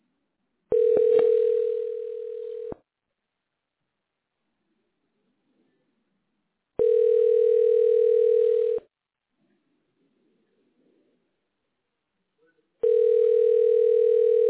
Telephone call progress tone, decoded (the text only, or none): ringback tone